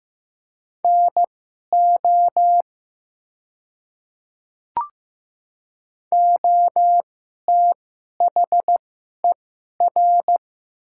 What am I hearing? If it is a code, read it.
Morse code, heard as a keyed tone, about 15 wpm, spaced as for 11 wpm, 700 Hz. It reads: NO OTHER
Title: 2x Character Spacing - Top 100 Words - 15wpm